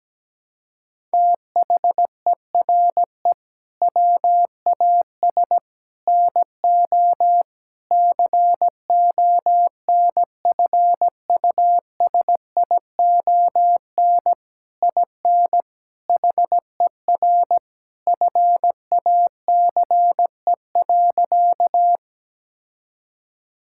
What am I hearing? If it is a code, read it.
Morse code, heard as a keyed tone, about 17 wpm, 700 Hz. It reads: THERE WAS NO CONFUSION IN HER FACE.
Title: Sentences from Top 1600 Words 17wpm